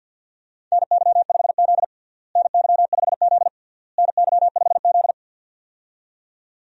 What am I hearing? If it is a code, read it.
Morse code, heard as a keyed tone, about 51 wpm, 700 Hz. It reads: NY5Z NY5Z NY5Z
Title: USA Call Signs - 4 Characters - Repeated 3x - 50wpm